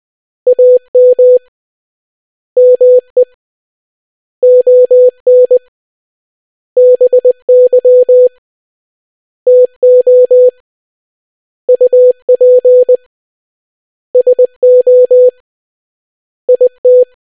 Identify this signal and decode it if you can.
Morse code, heard as a keyed tone, about 20 wpm, 500 Hz. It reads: AM ME ON BY TO UP SO IT